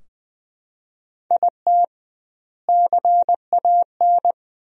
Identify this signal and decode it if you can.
Morse code, heard as a keyed tone, about 20 wpm, 700 Hz. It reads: IT CAN
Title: 2x Word-Spacing - Sentences from Top 400 Words - 20wpm